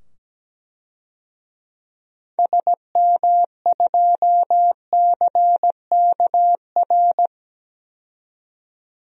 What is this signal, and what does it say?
Morse code, heard as a keyed tone, about 17 wpm, 700 Hz. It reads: SM2CKR
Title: DXCC-100 Call Signs - 6 Characters - 17wpm